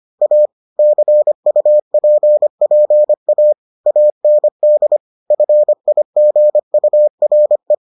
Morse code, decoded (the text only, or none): A CUPPA AND FIGURE